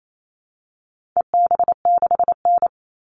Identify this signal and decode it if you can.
Morse code, heard as a keyed tone, about 28 wpm, 700 Hz. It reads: EB6N
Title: All Cut Numbers - 4-Digit 28wpm